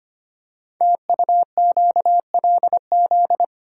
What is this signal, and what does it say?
Morse code, heard as a keyed tone, about 25 wpm, 700 Hz. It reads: TUQLZ
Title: Code Groups - 5 Letters - 25wpm